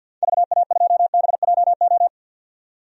Morse code, heard as a keyed tone, about 46 wpm, 700 Hz. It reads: VA2BJO
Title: DXCC-100 Call Signs - 6 Characters - 50wpm